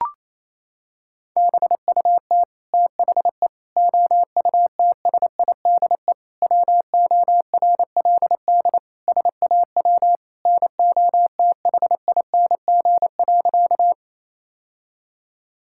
Morse code, heard as a keyed tone, about 27 wpm, 700 Hz. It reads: BUT THE OUTSIDE WORLD SAW NOTHING.
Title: Sentences from Top 1000 Words - Review Part 6 - 28wpm